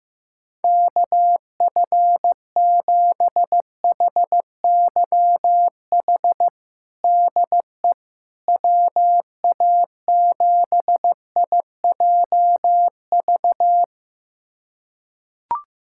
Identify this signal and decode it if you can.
Morse code, heard as a keyed tone, about 15 wpm, 700 Hz. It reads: KF7HYH DE WA7IJV
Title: USA Callsign DE Callsign - 6 Characters - 15wpm